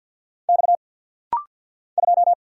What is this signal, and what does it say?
Morse code, heard as a keyed tone, about 50 wpm, 690 Hz. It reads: X 2
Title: Single Letter-Number Alt - Rapid-Fire 50wpm